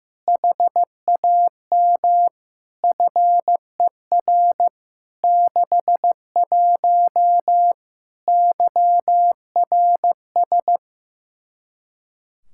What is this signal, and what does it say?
Morse code, heard as a keyed tone, about 15 wpm, 700 Hz. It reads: HAM FER 61 YRS
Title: Practice QSO - Exchange 3 - More Info - 15wpm